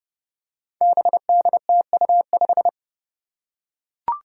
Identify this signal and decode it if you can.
Morse code, heard as a keyed tone, about 30 wpm, 700 Hz. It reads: BDTU5
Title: Five Challenging Letter-Number Combinations 30wpm